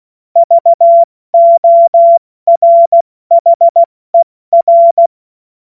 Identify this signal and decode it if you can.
Morse code, heard as a keyed tone, about 16 wpm, 680 Hz. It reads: VORHER
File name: Morsekurs / Witze / Witz 07. Dezember 2021, 70 F80